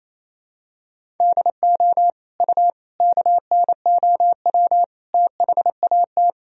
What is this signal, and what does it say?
Morse code, heard as a keyed tone, about 28 wpm, 700 Hz. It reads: DO U KNOW THAT
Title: Sentences from Top 400 Words and Abbreviations 28wpm